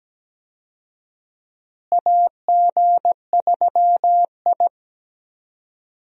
Morse code, heard as a keyed tone, about 17 wpm, 700 Hz. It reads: AG3I
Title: USA Call Signs - 4 Characters - Repeated 2x - 17wpm